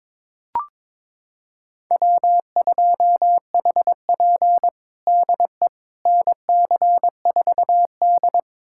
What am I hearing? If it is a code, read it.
Morse code, heard as a keyed tone, about 22 wpm, 700 Hz. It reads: W2HP DE NC4D
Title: USA Callsign DE Callsign - 4 Characters - 22wpm